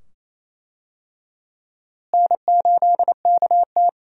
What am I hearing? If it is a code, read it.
Morse code, heard as a keyed tone, about 28 wpm, 700 Hz. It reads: N8KT
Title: USA Call Signs - 4 Characters - Repeated 2x - 28wpm